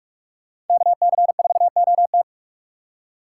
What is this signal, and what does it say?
Morse code, heard as a keyed tone, about 42 wpm, 700 Hz. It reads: KC4OT